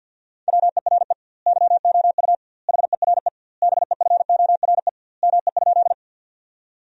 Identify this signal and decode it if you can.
Morse code, heard as a keyed tone, about 50 wpm, 700 Hz. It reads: WERE YOU HERE BEFORE ME?
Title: 2x Word-Spacing - Sentences from Top 200 Words - 50wpm